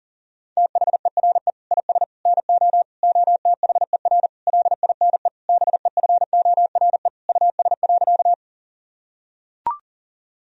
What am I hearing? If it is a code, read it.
Morse code, heard as a keyed tone, about 37 wpm, 700 Hz. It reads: THERE IS NO OTHER LINE BEFORE US.
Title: Sentences from Top 1000 Words - Review Part 1 - 40wpm